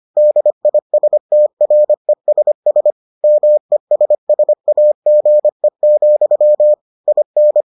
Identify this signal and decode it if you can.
Morse code, heard as a keyed tone, about 25 wpm, 600 Hz. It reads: DISTRESS MESSAGE, IN